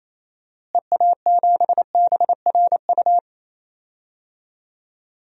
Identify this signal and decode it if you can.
Morse code, heard as a keyed tone, about 28 wpm, 700 Hz. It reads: EA7BRU
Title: DXCC-100 Call Signs - 6 Characters - Repeated 3x - 28wpm